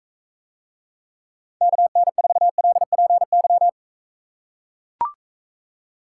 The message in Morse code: KN4LPY

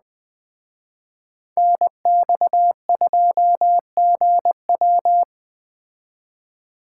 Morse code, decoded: NX2GW

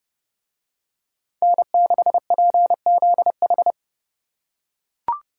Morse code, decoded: N6PZH

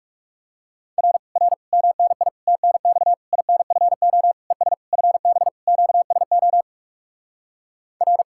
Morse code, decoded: R R MNI TNX INFO ES FB QSO R

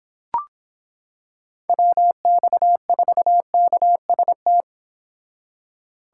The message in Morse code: WX4KST

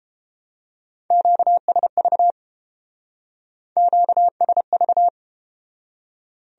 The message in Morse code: QSV QSV

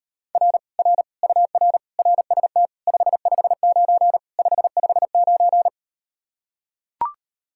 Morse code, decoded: R R UR RST 559 559

